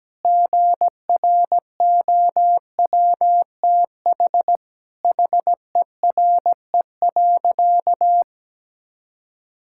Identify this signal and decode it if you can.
Morse code, heard as a keyed tone, about 17 wpm, 700 Hz. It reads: GROWTH HERE.
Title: Sentences from Top 1000 Words - Review Part 5 - 17wpm